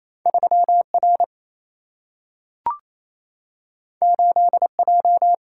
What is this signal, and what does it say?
Morse code, heard as a keyed tone, about 28 wpm, 700 Hz. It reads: 3 R 8 J
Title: Single-Letter Single-Number 28wpm